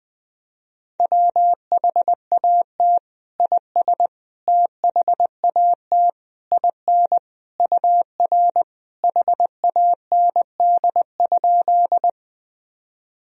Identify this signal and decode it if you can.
Morse code, heard as a keyed tone, about 20 wpm, 700 Hz. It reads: WHAT IS THAT IN UR HAND?
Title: Sentences from Top 200 Words and Abbreviations 20wpm